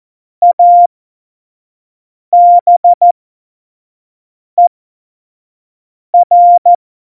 Morse code, heard as a keyed tone, about 14 wpm, spaced as for 5 wpm, 700 Hz. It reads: ABER
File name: Morsekurs / Witze / Witz 12. Februar 2019, 25 F70